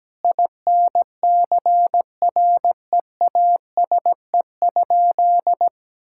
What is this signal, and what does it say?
Morse code, heard as a keyed tone, about 17 wpm, 700 Hz. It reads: INCREASE?
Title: Sentences from Top 1000 Words - Review Part 5 - 17wpm